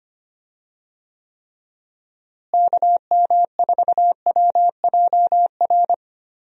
Morse code, heard as a keyed tone, about 25 wpm, 700 Hz. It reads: KM4WJR
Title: USA Call Signs - 6 Characters - Repeated 2x - 25wpm